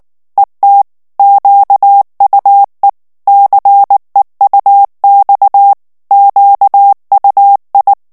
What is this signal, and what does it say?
Morse code, heard as a keyed tone, about 19 wpm, 800 Hz. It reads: ET QUE CEUX QUI
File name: CW / FICHIERS-MP3 / FRANCAIS TEXTE CLAIR 2 VITESSE 18 MOTS MINUTES